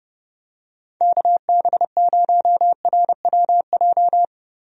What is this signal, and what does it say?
Morse code, heard as a keyed tone, about 30 wpm, 700 Hz. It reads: KB0RWJ
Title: USA Call Signs - 6 Characters - Repeated 2x - 30wpm